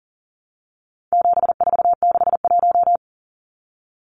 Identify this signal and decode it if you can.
Morse code, heard as a keyed tone, about 40 wpm, 700 Hz. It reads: 7461
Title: Numbers - 4-Digit 40wpm